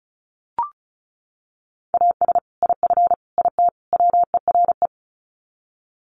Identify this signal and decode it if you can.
Morse code, heard as a keyed tone, about 34 wpm, 700 Hz. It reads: AS IF IT WERE
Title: Sets of 4 Words 35wpm